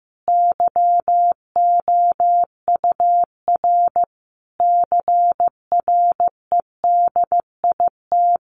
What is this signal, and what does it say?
Morse code, heard as a keyed tone, about 15 wpm, 700 Hz. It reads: YOUR CREDIT
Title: Sentences from Top 1400 Words 15wpm